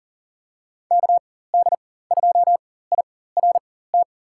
Morse code, heard as a keyed tone, about 40 wpm, spaced as for 19 wpm, 700 Hz. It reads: KD2IRT
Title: USA Call Signs - 4x Character Spacing - 40wpm